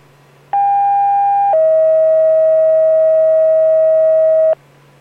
Two-tone page, heard as a one-second tone, then a three-second tone, about 800 then 640 Hz. It is heard from 0.5 to 4.5 s.